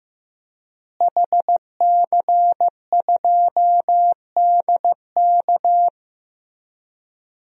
Morse code, heard as a keyed tone, about 15 wpm, 700 Hz. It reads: HC2DK
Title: DXCC-100 Call Signs - 5 Characters - Repeated 3x - 15wpm